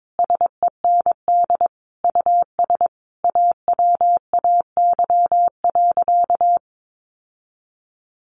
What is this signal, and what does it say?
Morse code, heard as a keyed tone, about 22 wpm, 700 Hz. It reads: SEND US AWAY.
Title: Sentences from Top 1500 Words 22wpm